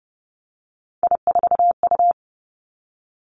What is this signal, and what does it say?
Morse code, heard as a keyed tone, about 30 wpm, 700 Hz. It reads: I4U
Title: Three Challenging Letter-Number Combinations 30wpm